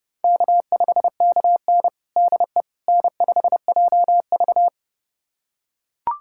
Morse code, heard as a keyed tone, about 32 wpm, 700 Hz. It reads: K5KN DE N5JV
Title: USA Callsign DE Callsign - 4 Characters - 30wpm